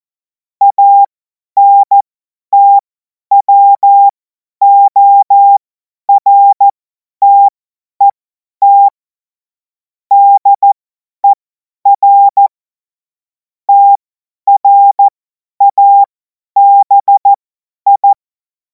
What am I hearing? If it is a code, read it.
Morse code, heard as a keyed tone, about 14 wpm, spaced as for 10 wpm, 800 Hz. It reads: ANTWORTET DER TRABI
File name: Morsekurs / Witze / Witz 21. April 2026 50 F70